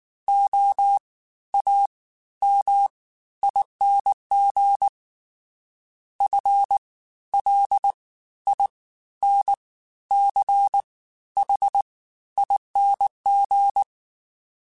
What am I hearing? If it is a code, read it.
Morse code, heard as a keyed tone, about 19 wpm, spaced as for 12 wpm, 800 Hz. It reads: OAMING FLINCHING